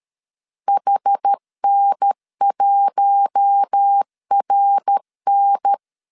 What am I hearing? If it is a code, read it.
Morse code, heard as a keyed tone, about 13 wpm, 800 Hz. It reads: HN1RN
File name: CW / DET / 1 6 1500C S B 12wpm 3431